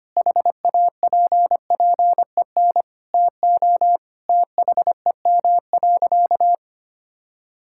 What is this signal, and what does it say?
Morse code, heard as a keyed tone, about 25 wpm, 700 Hz. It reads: HAPPEN TO THEM.
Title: Sentences from Top 1000 Words - Review Part 5 - 25wpm